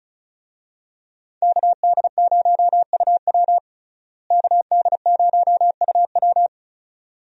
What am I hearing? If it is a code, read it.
Morse code, heard as a keyed tone, about 36 wpm, 700 Hz. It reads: KD0UW KD0UW
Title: USA Call Signs - 5 Characters - Repeated 2x - 35wpm